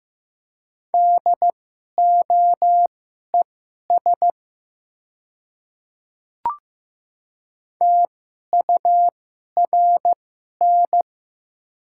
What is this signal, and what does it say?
Morse code, heard as a keyed tone, about 15 wpm, spaced as for 11 wpm, 690 Hz. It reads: DOES TURN
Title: 2x Character Spacing - Top 300 Words - 15wpm